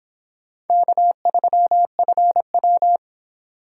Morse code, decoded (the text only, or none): K3FW